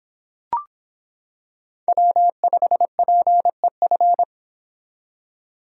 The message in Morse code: W5PEF

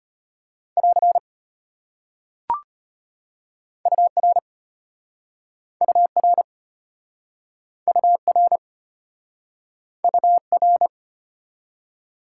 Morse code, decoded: <AR> UR UR UR UR

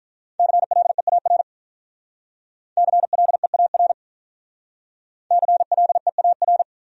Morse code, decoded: CLEAR CLEAR CLEAR

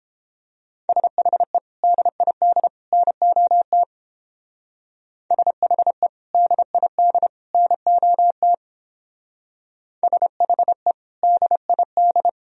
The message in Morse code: SHE DID NOT SHE DID NOT SHE DID

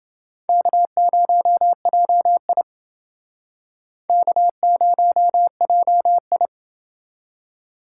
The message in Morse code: K0JI K0JI

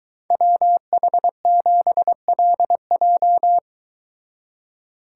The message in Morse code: WH7LJ

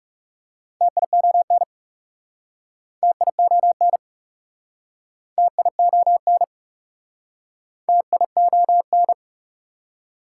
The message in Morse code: TION TION TION TION

